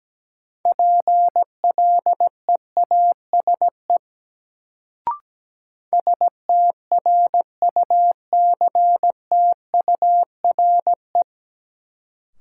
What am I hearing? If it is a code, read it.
Morse code, heard as a keyed tone, about 17 wpm, 700 Hz. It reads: PLEASE STRUCTURE
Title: Sentences from Top 1500 Words - New Word Drill Alt - 17wpm